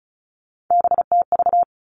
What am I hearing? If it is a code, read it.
Morse code, heard as a keyed tone, about 34 wpm, 700 Hz. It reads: BTV
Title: All Cut Numbers - 3-Digit 35wpm